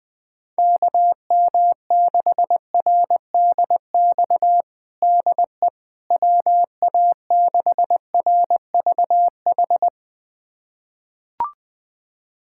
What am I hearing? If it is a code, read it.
Morse code, heard as a keyed tone, about 20 wpm, 700 Hz. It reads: KM6RDX DE WA6RVH
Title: USA Callsign DE Callsign - 6 Characters - 20wpm